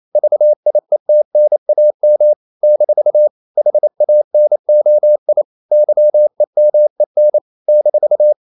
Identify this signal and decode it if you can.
Morse code, heard as a keyed tone, about 28 wpm, 600 Hz. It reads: VIETNAM - HANOI YEMEN -
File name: CW / Practice / NationalCapitals28